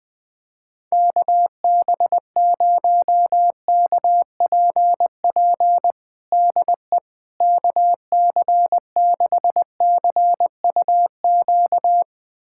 Morse code, heard as a keyed tone, about 21 wpm, 700 Hz. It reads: KB0KPP DE KC6CUQ
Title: USA Callsign DE Callsign 20wpm